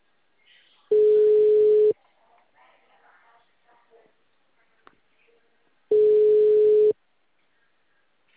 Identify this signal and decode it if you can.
Ringback tone (Europe, 425 Hz), with speech.
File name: out- -2007- - - .74389